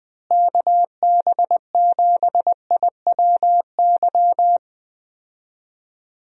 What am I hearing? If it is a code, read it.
Morse code, heard as a keyed tone, about 20 wpm, 700 Hz. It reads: KB7IWY
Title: USA Call Signs - 6 Characters - Repeated 3x - 20wpm